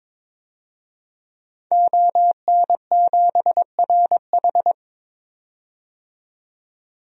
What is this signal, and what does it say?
Morse code, heard as a keyed tone, about 21 wpm, 700 Hz. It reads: ON7RH
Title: DXCC-100 Call Signs - 5 Characters - 22wpm